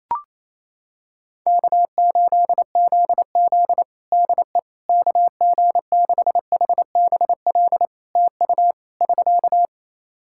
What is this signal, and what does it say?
Morse code, heard as a keyed tone, about 28 wpm, 700 Hz. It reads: K8ZZ DE KG6HBL TU <SK>
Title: Practice QSO - Exchange 4 - Ending - 28wpm